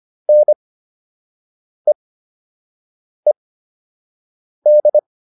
Morse code, heard as a keyed tone, about 25 wpm, spaced as for 6 wpm, 600 Hz. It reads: NEED